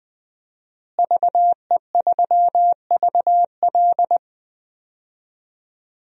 Morse code, heard as a keyed tone, about 20 wpm, 700 Hz. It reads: VE3VL
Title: DXCC-100 Call Signs - 5 Characters - Repeated 2x - 20wpm